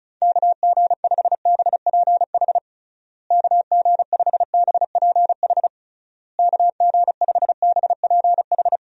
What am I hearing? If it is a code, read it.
Morse code, heard as a keyed tone, about 34 wpm, 700 Hz. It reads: KG5BPH KG5BPH KG5BPH
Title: USA Call Signs - 6 Characters - Repeated 3x - 35wpm